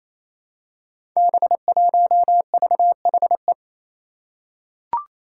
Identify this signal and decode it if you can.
Morse code, heard as a keyed tone, about 28 wpm, 700 Hz. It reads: B1VHE
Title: Five Challenging Letter-Number Combinations 28wpm